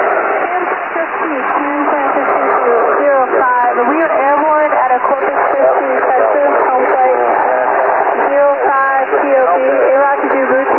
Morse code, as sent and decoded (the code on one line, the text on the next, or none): none